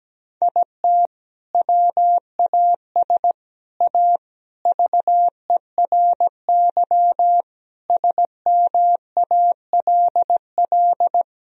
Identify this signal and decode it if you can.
Morse code, heard as a keyed tone, about 17 wpm, 700 Hz. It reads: IT WAS A VERY SMALL
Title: Sentences from Top 1000 Words - Review Part 1 - 17wpm